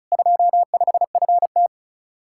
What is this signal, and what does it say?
Morse code, heard as a keyed tone, about 43 wpm, 700 Hz. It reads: 25FT
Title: Speed-Racing Common QSO Phrases - 35wpm